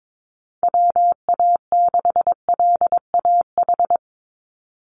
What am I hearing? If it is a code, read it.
Morse code, heard as a keyed tone, about 22 wpm, 700 Hz. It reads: WA6LAH
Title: USA Call Signs 22wpm